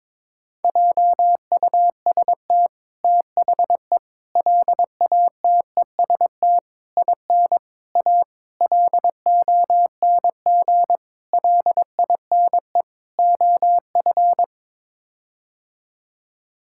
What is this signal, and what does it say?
Morse code, heard as a keyed tone, about 22 wpm, 700 Hz. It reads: JUST THE LATEST IN A LONG LINE OF
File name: practice / 22wpm / Sets of 8 Words - 22wpm